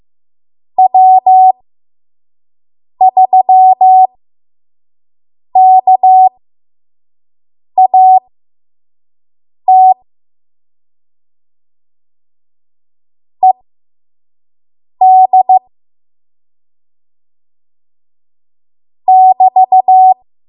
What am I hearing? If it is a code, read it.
Morse code, heard as a keyed tone, about 15 wpm, spaced as for 5 wpm, 750 Hz. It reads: W3KAT ED <BT>
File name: Morse / 05WPM